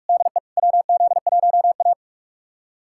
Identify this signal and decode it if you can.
Morse code, heard as a keyed tone, about 44 wpm, 690 Hz. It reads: DE WZ1A
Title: USA Callsign DE Callsign - 4 Characters - 45wpm